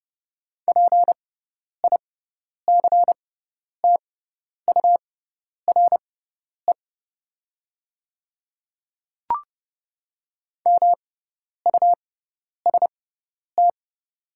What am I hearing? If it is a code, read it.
Morse code, heard as a keyed tone, about 30 wpm, spaced as for 10 wpm, 700 Hz. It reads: PICTURE MUST
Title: 6x Character Spacing - Top 200 Words - 30wpm